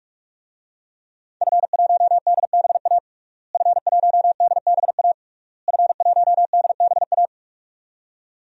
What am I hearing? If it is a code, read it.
Morse code, heard as a keyed tone, about 46 wpm, 700 Hz. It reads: F1DBA F1DBA F1DBA
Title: DXCC-100 Call Signs - 5 Characters - Repeated 3x - 45wpm